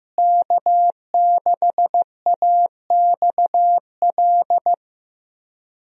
Morse code, heard as a keyed tone, about 15 wpm, 700 Hz. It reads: K6AXL
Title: USA Call Signs - 5 Characters - Repeated 3x - 15wpm